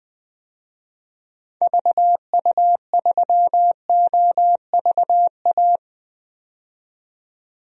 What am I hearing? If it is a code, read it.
Morse code, heard as a keyed tone, about 20 wpm, 700 Hz. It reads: VU3OVA
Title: DXCC-100 Call Signs - 6 Characters - 20wpm